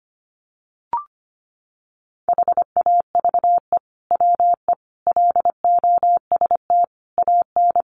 Morse code, heard as a keyed tone, about 25 wpm, 700 Hz. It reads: HAVE WE LOST AN
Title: Sentences from Top 1400 Words 25wpm